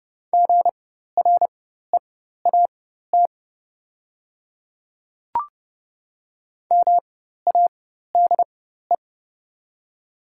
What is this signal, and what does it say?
Morse code, heard as a keyed tone, about 30 wpm, spaced as for 14 wpm, 700 Hz. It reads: GREAT MADE
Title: 4x Character Spacing - Top 200 Words - 30wpm